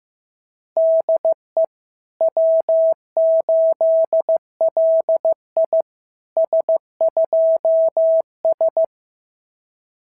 Morse code, heard as a keyed tone, about 14 wpm, 650 Hz. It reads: DE W8LI S2S
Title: SOTA QSO Practice 15wpm